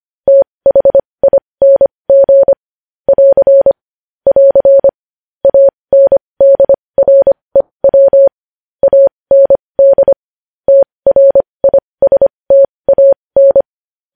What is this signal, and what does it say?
Morse code, heard as a keyed tone, about 25 wpm, spaced as for 20 wpm, 550 Hz. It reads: THING <AR> <AR> ANDREW AND TRISTAN
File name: podcast / News headlines-1x-25x20